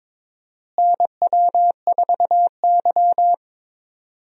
Morse code, heard as a keyed tone, about 22 wpm, 700 Hz. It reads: NW4Y
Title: USA Call Signs - 4 Characters - Repeated 2x - 22wpm